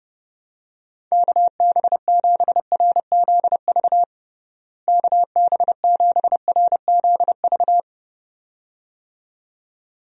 Morse code, heard as a keyed tone, about 32 wpm, 700 Hz. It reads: KB7RZV KB7RZV
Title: USA Call Signs - 6 Characters - Repeated 2x - 30wpm